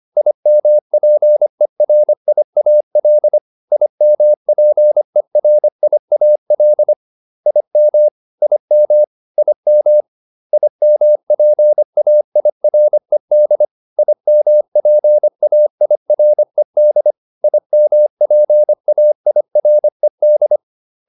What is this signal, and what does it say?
Morse code, heard as a keyed tone, about 25 wpm, 600 Hz. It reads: IMPERIAL IMPERIAL IM IM IM IMPAIRED IMPAIRED IMPAIRED